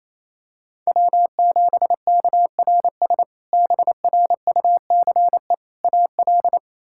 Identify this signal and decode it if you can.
Morse code, heard as a keyed tone, about 28 wpm, 700 Hz. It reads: W7KRS BRUCE AL
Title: K1USN SST QSO Practice - 28wpm